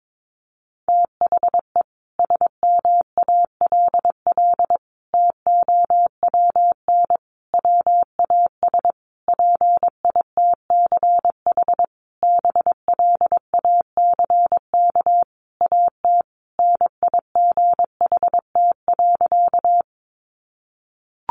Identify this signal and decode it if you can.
Morse code, heard as a keyed tone, about 22 wpm, 700 Hz. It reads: THE SMALL TOWN WAS PITCH BLACK AT NIGHT.
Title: Sentences from Top 1300 Words 22wpm